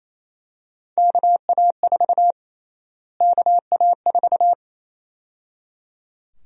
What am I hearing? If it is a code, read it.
Morse code, heard as a keyed tone, about 28 wpm, 700 Hz. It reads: KA4 KA4
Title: USA Call Sign Prefixes - Repeated 2x - 28wpm